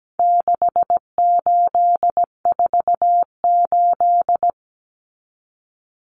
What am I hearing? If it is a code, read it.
Morse code, heard as a keyed tone, about 17 wpm, 700 Hz. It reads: 6848
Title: Numbers - 4-Digit 17wpm